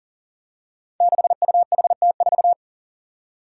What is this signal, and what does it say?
Morse code, heard as a keyed tone, about 40 wpm, 700 Hz. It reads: 6UHT4
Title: Five Challenging Letter-Number Combinations 40wpm